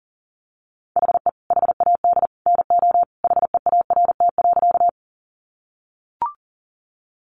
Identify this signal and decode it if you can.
Morse code, heard as a keyed tone, about 40 wpm, 700 Hz. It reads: HE HAD NO HEART.
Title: Sentences from Top 1200 Words 40wpm